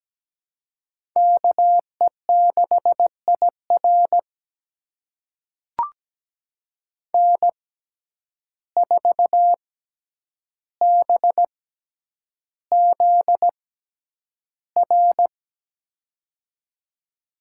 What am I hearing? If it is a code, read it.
Morse code, heard as a keyed tone, about 17 wpm, spaced as for 9 wpm, 700 Hz. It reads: KE6IR N4BZR
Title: USA Call Signs - 5 Characters - 6x Character Spacing - 17wpm